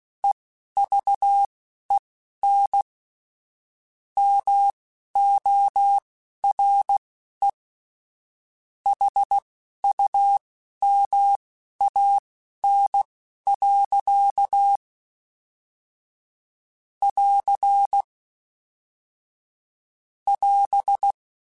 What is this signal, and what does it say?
Morse code, heard as a keyed tone, about 16 wpm, spaced as for 12 wpm, 800 Hz. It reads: EVEN MORE HUMAN. <AR> <AS>
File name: Chuckles / Chuckles-mp3s / HumanErr @10